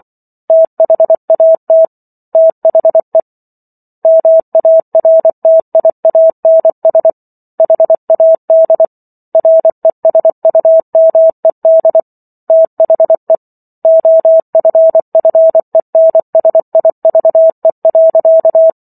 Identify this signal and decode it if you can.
Morse code, heard as a keyed tone, about 24 wpm, 650 Hz. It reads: THAT THE MARTIANS HAD RESUMED THE OFFENSIVE.